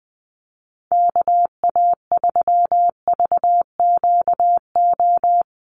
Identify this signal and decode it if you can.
Morse code, heard as a keyed tone, about 20 wpm, 700 Hz. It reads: KA3VQO
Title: USA Call Signs Repeated 2x 20wpm